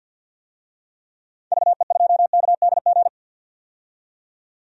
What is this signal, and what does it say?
Morse code, heard as a keyed tone, about 52 wpm, 700 Hz. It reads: VE2KDG
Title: DXCC-100 Call Signs - 6 Characters - 50wpm